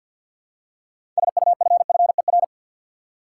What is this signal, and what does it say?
Morse code, heard as a keyed tone, about 52 wpm, 700 Hz. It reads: SUFFER
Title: Six Letter Words - 50wpm